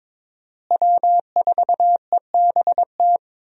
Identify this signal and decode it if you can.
Morse code, heard as a keyed tone, about 22 wpm, 700 Hz. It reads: W4EBT
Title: USA Call Signs - 5 Characters - Repeated 3x - 22wpm